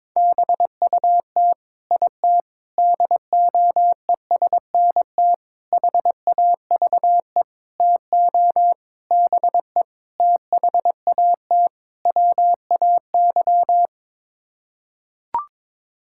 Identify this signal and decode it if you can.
Morse code, heard as a keyed tone, about 22 wpm, 700 Hz. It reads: BUT IT DOESNT HAVE TO BE THAT WAY
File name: practice / 22wpm / Sets of 8 Words - 22wpm